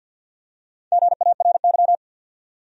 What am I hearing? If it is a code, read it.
Morse code, heard as a keyed tone, about 49 wpm, 700 Hz. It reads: GARY
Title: USA Common Names - Max 4 Letters - 50wpm